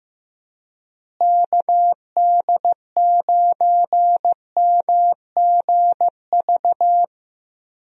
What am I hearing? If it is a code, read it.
Morse code, heard as a keyed tone, about 15 wpm, 700 Hz. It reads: KD9MGV
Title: USA Call Signs - 6 Characters - Repeated 2x - 15wpm